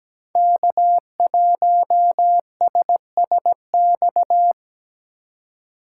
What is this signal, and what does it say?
Morse code, heard as a keyed tone, about 17 wpm, 700 Hz. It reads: K1SSX
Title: Code Groups - 5 Letter:Numbers - 17wpm